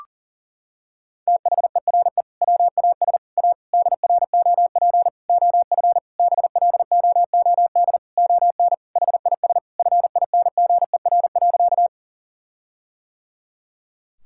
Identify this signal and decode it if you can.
Morse code, heard as a keyed tone, about 37 wpm, 700 Hz. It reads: THERE WAS A DROP OF BLOOD ON HIS FINGER.
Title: Sentences from Top 1600 Words 40wpm